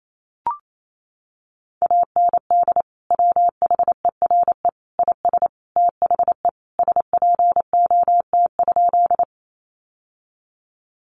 Morse code, heard as a keyed tone, about 28 wpm, 700 Hz. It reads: AND WHERE IS THE SPOT?
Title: Sentences from Top 1100 Words 28wpm